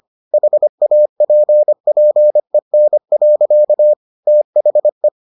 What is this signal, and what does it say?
Morse code, heard as a keyed tone, about 25 wpm, 600 Hz. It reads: HAPPEN. THE